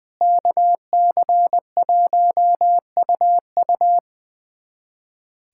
Morse code, heard as a keyed tone, about 21 wpm, 700 Hz. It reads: KC1UU
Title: USA Callsign DE Callsign - 5 Characters - 20wpm